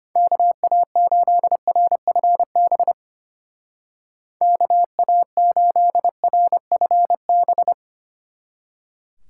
Morse code, heard as a keyed tone, about 32 wpm, 700 Hz. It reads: KA8RFB KA8RFB